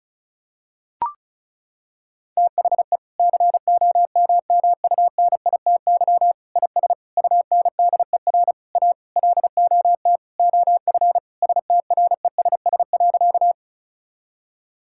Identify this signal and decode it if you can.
Morse code, heard as a keyed tone, about 34 wpm, 700 Hz. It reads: THE COMMUNITY IS UNDER A LOT OF STRESS.
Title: Sentences from Top 1000 Words - Review Part 6 - 35wpm